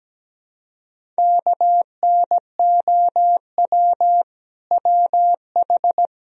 Morse code, heard as a keyed tone, about 20 wpm, 700 Hz. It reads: KNOW WH
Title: Speed-Racing - Sets of 2 Words 17wpm